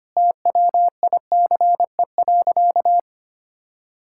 Morse code, heard as a keyed tone, about 25 wpm, 700 Hz. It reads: TWICE.
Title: Sentences from Top 1600 Words 25wpm